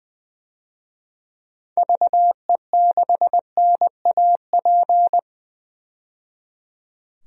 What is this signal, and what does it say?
Morse code, heard as a keyed tone, about 20 wpm, 700 Hz. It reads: VE6NAP
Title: DXCC-100 Call Signs - 6 Characters - 20wpm